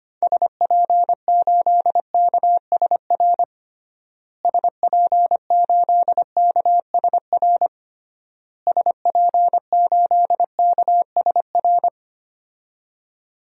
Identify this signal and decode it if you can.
Morse code, heard as a keyed tone, about 25 wpm, 690 Hz. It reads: SP8KSR SP8KSR SP8KSR
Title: DXCC-100 Call Signs - 6 Characters - Repeated 3x - 25wpm